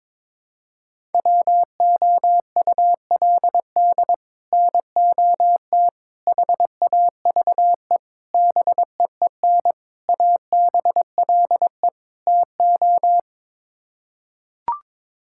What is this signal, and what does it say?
Morse code, heard as a keyed tone, about 22 wpm, 700 Hz. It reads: WOULD NOT HAVE BEEN ABLE TO
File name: practice / 22wpm / Sets of 6 Words - 22wpm